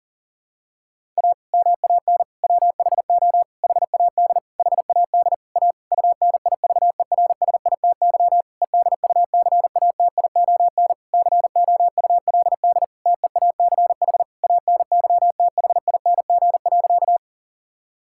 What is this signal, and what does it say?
Morse code, heard as a keyed tone, about 40 wpm, 700 Hz. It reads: A MAN WHO HAD HAD A UNIVERSITY EDUCATION COULD TEACH ANYTHING.
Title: Sentences from Top 1000 Words - Review Part 6 - 40wpm